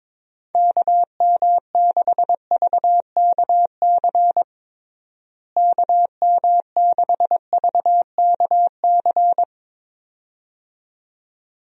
Morse code, heard as a keyed tone, about 22 wpm, 700 Hz. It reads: KM6VKC KM6VKC
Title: USA Call Signs - 6 Characters - Repeated 2x - 22wpm